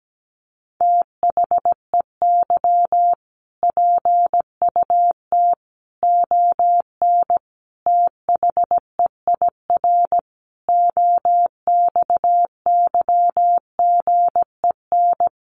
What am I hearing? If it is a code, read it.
Morse code, heard as a keyed tone, about 17 wpm, 700 Hz. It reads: THEY PUT ON THEIR OXYGEN
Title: A Sound Of Thunder 17wpm